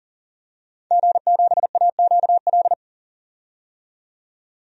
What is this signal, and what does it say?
Morse code, heard as a keyed tone, about 40 wpm, 700 Hz. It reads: G7AQL
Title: DXCC-100 Call Signs - 5 Characters - 40wpm